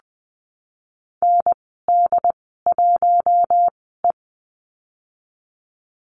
Morse code, heard as a keyed tone, about 20 wpm, spaced as for 15 wpm, 700 Hz. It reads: ND1E
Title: 076 Lesson 13g - Letter L - Call Signs 20wpm